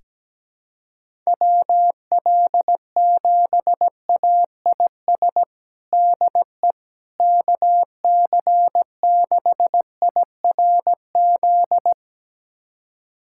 Morse code, heard as a keyed tone, about 17 wpm, 700 Hz. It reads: WL7AIS DE KC6IRZ
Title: USA Callsign DE Callsign 17wpm